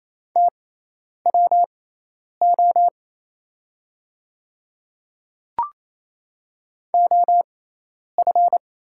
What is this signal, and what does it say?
Morse code, heard as a keyed tone, about 28 wpm, spaced as for 10 wpm, 700 Hz. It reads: TWO OF
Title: 6x Character Spacing - Top 100 Words - 28wpm